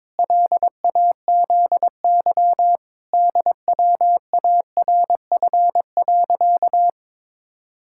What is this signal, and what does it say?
Morse code, heard as a keyed tone, about 22 wpm, 700 Hz. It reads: LAZY DWARF.
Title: Pangram Challenge v2 - 22wpm